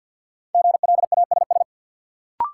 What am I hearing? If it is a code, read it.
Morse code, heard as a keyed tone, about 53 wpm, 700 Hz. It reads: GLASS
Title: Sentences from Top 1000 Words - Review Part 3 - New Word Drill - 50wpm